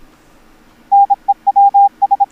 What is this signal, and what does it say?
Morse code, heard as a keyed tone, about 27 wpm, 800 Hz. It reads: NEWS